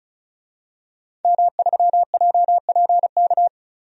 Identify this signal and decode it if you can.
Morse code, heard as a keyed tone, about 36 wpm, 700 Hz. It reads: M3JPK
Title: DXCC-100 Call Signs - 5 Characters - 35wpm